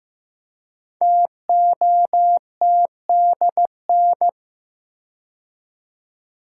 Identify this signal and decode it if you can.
Morse code, heard as a keyed tone, about 15 wpm, 700 Hz. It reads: TOTDN
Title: Code Groups - 5 Letters - 15wpm